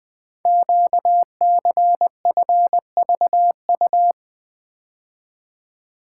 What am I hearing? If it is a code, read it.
Morse code, heard as a keyed tone, about 21 wpm, 700 Hz. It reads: QCFVU
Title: Code Groups - 5 Letters - 20wpm